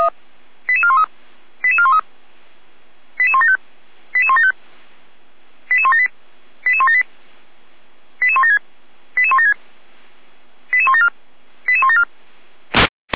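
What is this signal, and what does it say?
Five-tone selective call (ZVEI), heard as quick runs of five tones, tones of about 70 ms. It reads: 88413 88413 88175 88175 88178 88178 88176 88176 88174 88174
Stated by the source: SDR-received ZVEI five-tone selective call, code 88156.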